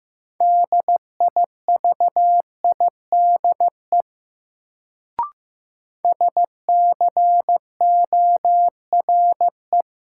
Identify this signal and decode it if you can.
Morse code, heard as a keyed tone, about 15 wpm, 700 Hz. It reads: DIVIDE SCORE
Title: Sentences from Top 1600 Words - New Word Drill Alt - 15wpm